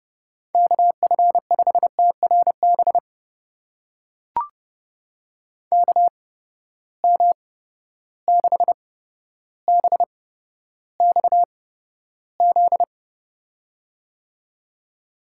Morse code, heard as a keyed tone, about 30 wpm, spaced as for 13 wpm, 700 Hz. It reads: KF5TRB KM6BXZ
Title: USA Call Signs - 6 Characters - 8x Character Spacing - 30wpm